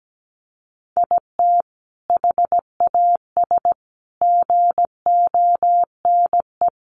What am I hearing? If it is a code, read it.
Morse code, heard as a keyed tone, about 17 wpm, 700 Hz. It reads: IT HAS GONE
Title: Sentences from Top 1200 Words 17wpm